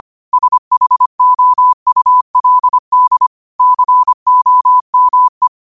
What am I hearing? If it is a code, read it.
Morse code, heard as a keyed tone, about 25 wpm, 1000 Hz. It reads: SHOULD COME
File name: qotdfiles / 25wpm